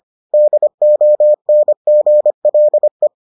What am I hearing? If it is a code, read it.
Morse code, heard as a keyed tone, about 25 wpm, 600 Hz. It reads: DONGLE